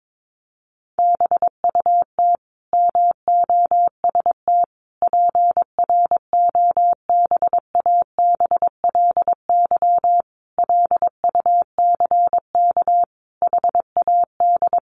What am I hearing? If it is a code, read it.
Morse code, heard as a keyed tone, about 22 wpm, 700 Hz. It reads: BUT MOST PROBABLY LUCK HAD
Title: Sentences from Top 1500 Words 22wpm